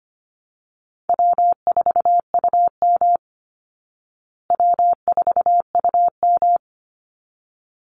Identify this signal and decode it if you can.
Morse code, heard as a keyed tone, about 25 wpm, 700 Hz. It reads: W4UM W4UM
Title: Easy Call Signs Repeated 2x - 25wpm